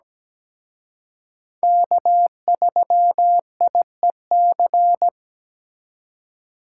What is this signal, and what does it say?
Morse code, heard as a keyed tone, about 20 wpm, 700 Hz. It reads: K3IEC
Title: Speed Racing - USA 1x3 Call Signs 17wpm